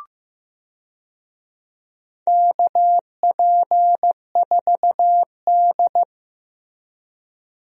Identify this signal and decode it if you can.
Morse code, heard as a keyed tone, about 15 wpm, 700 Hz. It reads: KP4D
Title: DXCC-100 Call Signs - 4 Characters - Repeated 2x - 15wpm